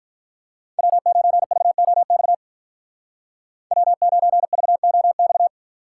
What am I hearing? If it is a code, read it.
Morse code, heard as a keyed tone, about 43 wpm, 700 Hz. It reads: W9VOX W9VOX
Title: Speed Racing - USA 1x3 Call Signs 35wpm